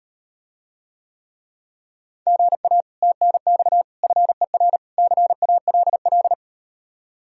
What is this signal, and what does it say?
Morse code, heard as a keyed tone, about 38 wpm, 700 Hz. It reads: GA TNX FER CALL